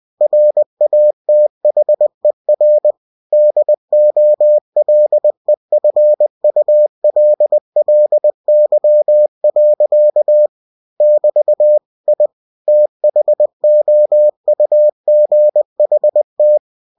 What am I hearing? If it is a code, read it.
Morse code, heard as a keyed tone, about 20 wpm, 600 Hz. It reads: RATHER DOLEFULLY. <BT> I THOUGHT
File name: cw-books / winnie-the-pooh / winnie-20wpm-page-0108